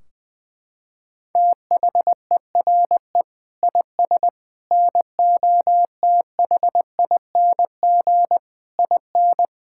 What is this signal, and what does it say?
Morse code, heard as a keyed tone, about 20 wpm, 700 Hz. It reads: THERE IS NOTHING IN
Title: Sentences from Top 1600 Words 20wpm